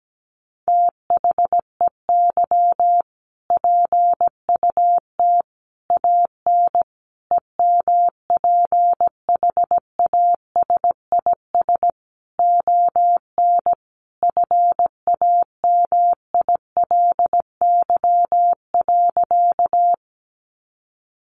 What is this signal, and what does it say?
Morse code, heard as a keyed tone, about 17 wpm, 700 Hz. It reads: THEY PUT AN EMPHASIS ON FAMILY.
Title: Sentences from Top 1400 Words 17wpm